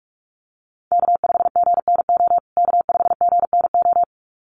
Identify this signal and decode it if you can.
Morse code, heard as a keyed tone, about 44 wpm, 700 Hz. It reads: K5GNO K5GNO
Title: USA Call Signs Repeated 2x 45wpm